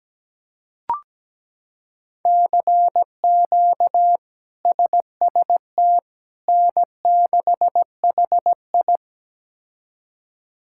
Morse code, heard as a keyed tone, about 17 wpm, 700 Hz. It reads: CQ SST N6HI
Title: K1USN SST QSO Practice - 17wpm